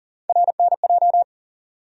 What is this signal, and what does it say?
Morse code, heard as a keyed tone, about 40 wpm, 700 Hz. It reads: RNJ
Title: USA Call Sign Postfixes - 3 Letters 40wpm